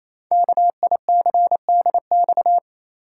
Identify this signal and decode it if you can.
Morse code, heard as a keyed tone, about 27 wpm, 700 Hz. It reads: KICDX
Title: Code Groups - 5 Letter:Numbers - 28wpm